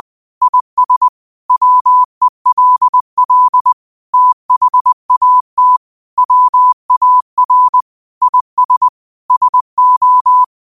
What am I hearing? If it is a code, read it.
Morse code, heard as a keyed tone, about 20 wpm, 1000 Hz. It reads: IS WELL THAT WAR IS SO
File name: qotdfiles / 20wpm